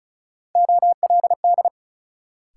Morse code, heard as a keyed tone, about 36 wpm, 700 Hz. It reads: OLD
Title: Sentences from Top 200 Words - New Word Drill - 35wpm